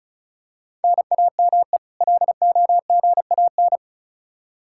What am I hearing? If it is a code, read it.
Morse code, heard as a keyed tone, about 36 wpm, 700 Hz. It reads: NAME LOGAN
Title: Practice QSO - Exchange 1 - Beginning - 35wpm